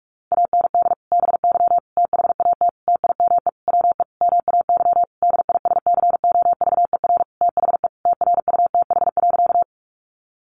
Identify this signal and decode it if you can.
Morse code, heard as a keyed tone, about 44 wpm, 700 Hz. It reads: AND BY THAT TIME WE MAY DISCOVER THE TRUTH.
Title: Sentences from Top 1100 Words 45wpm